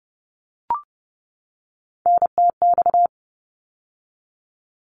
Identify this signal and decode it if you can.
Morse code, heard as a keyed tone, about 30 wpm, 700 Hz. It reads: NTX
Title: ARRL Sections 30wpm